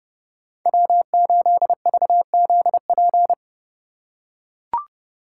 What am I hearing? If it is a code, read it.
Morse code, heard as a keyed tone, about 29 wpm, 700 Hz. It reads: W8VZP